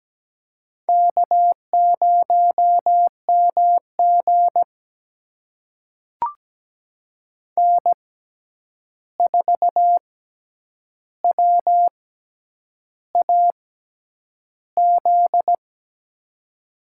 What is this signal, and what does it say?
Morse code, heard as a keyed tone, about 17 wpm, spaced as for 8 wpm, 700 Hz. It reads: K0MG N4WAZ
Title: USA Call Signs - 6x Character Spacing - 17wpm